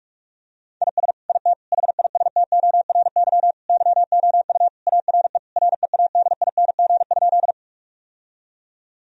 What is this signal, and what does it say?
Morse code, heard as a keyed tone, about 46 wpm, 700 Hz. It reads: IS IT HISTORY YOU ARE READING?
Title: Sentences from Top 1000 Words - Review Part 3 - 45wpm